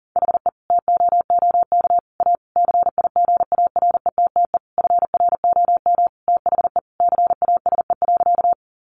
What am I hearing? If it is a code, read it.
Morse code, heard as a keyed tone, about 37 wpm, 700 Hz. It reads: HE TOOK A CIGARETTE FROM THE CASE.
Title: Sentences from Top 1000 Words 40wpm